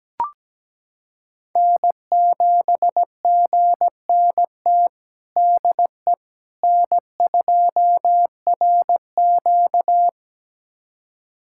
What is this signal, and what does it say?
Morse code, heard as a keyed tone, about 17 wpm, 700 Hz. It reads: N7GNT DE N2RQ
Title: USA Callsign DE Callsign 17wpm